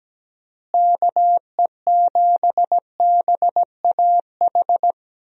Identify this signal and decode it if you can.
Morse code, heard as a keyed tone, about 17 wpm, 700 Hz. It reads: KE7BAH
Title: USA Call Signs - 6 Characters - Repeated 2x - 17wpm